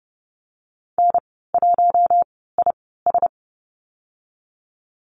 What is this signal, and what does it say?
Morse code, heard as a keyed tone, about 31 wpm, spaced as for 17 wpm, 700 Hz. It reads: N1IS
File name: Lesson 19g - Letter w - Call Signs 30wpm